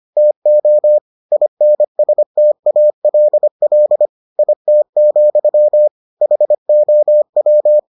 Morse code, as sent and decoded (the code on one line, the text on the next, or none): - --- / .. -. ... - .- .-.. .-.. / .. - --..-- / .... --- .--
TO INSTALL IT, HOW